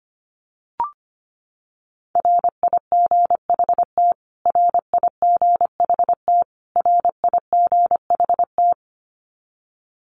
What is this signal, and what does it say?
Morse code, heard as a keyed tone, about 25 wpm, 700 Hz. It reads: RIGHT RIGHT RIGHT
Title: Top 500 Words Repeated 25wpm